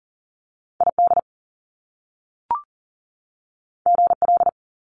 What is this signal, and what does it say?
Morse code, heard as a keyed tone, about 40 wpm, 700 Hz. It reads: ID GL
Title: Top Two Letter Combinations in Words 40wpm